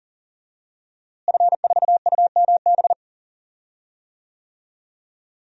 Morse code, heard as a keyed tone, about 37 wpm, 700 Hz. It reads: F4UMB
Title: DXCC-100 Call Signs - 5 Characters - Repeated 3x - 40wpm